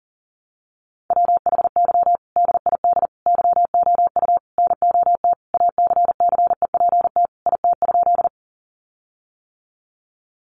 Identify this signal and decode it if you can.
Morse code, heard as a keyed tone, about 40 wpm, 700 Hz. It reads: WHY DID YOU NOT ACCEPT IT?
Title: Sentences from Top 1300 Words 40wpm